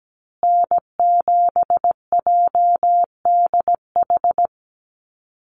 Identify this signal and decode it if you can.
Morse code, heard as a keyed tone, about 17 wpm, 700 Hz. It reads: N7JDH
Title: USA Call Signs Repeated 2x 17wpm